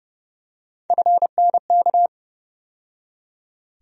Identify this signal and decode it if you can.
Morse code, heard as a keyed tone, about 30 wpm, 700 Hz. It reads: FNK
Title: USA Call Sign Postfixes - 3 Letters 30wpm